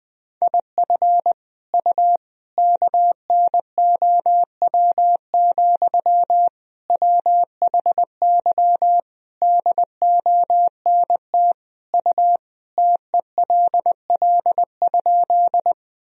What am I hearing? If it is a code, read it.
Morse code, heard as a keyed tone, about 20 wpm, 700 Hz. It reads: IF U KNOW, WHY DONT U TELL?
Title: Sentences from Top 300 Words and Abbreviations 20wpm